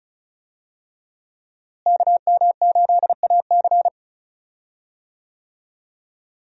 Morse code, heard as a keyed tone, about 36 wpm, 700 Hz. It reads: KM8AC
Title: USA Call Signs - 5 Characters - Repeated 3x - 35wpm